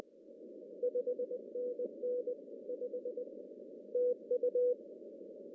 Morse code, heard as a keyed tone, about 20 wpm, 490 Hz. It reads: 5NN 5 TU
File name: CW / cqww-cw-k5tr